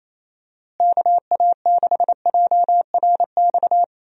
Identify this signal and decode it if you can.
Morse code, heard as a keyed tone, about 28 wpm, 700 Hz. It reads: KA6JRX
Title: USA Call Signs - 6 Characters - Repeated 3x - 28wpm